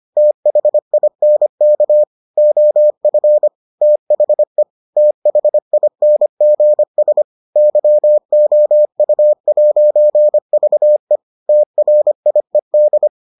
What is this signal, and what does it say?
Morse code, heard as a keyed tone, about 25 wpm, 600 Hz. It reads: THINK OF THE THINGS YOU'VE TRIED